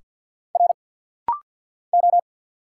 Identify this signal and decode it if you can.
Morse code, heard as a keyed tone, about 50 wpm, 690 Hz. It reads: R O
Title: Single Letters - Rapid-Fire 50wpm